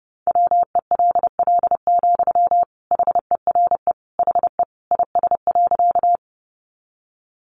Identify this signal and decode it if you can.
Morse code, heard as a keyed tone, about 30 wpm, 700 Hz. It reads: WELL, HERE HE IS.
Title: Sentences from Top 200 Words 30wpm